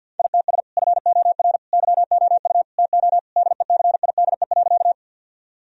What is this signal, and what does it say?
Morse code, heard as a keyed tone, about 51 wpm, 700 Hz. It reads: ITS FOR YOU TO DECIDE.
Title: Sentences from Top 1000 Words - Review Part 3 - 50wpm